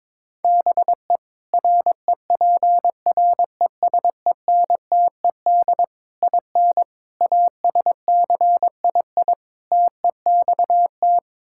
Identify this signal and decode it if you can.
Morse code, heard as a keyed tone, about 22 wpm, 700 Hz. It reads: BE REPRESENTED IN ASCII TEXT
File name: practice / 22wpm / Build A Phrase - 5-Word Phrases - 22wpm